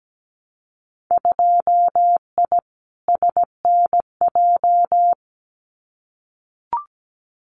Morse code, heard as a keyed tone, about 17 wpm, 700 Hz. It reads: 2I SNJ
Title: Winter Field Day Exchanges Encore 17wpm